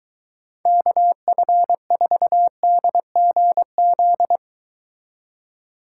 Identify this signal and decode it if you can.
Morse code, heard as a keyed tone, about 23 wpm, 700 Hz. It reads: KF4DGZ